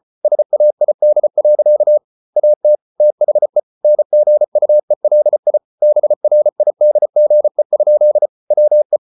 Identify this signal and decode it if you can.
Morse code, heard as a keyed tone, about 34 wpm, 600 Hz. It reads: SAID. AT THE NGUELI BRIDGE? WE